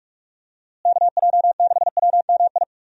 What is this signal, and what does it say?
Morse code, heard as a keyed tone, about 43 wpm, 700 Hz. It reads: KJ6WMI
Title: USA Call Signs - 6 Characters - Repeated 3x - 45wpm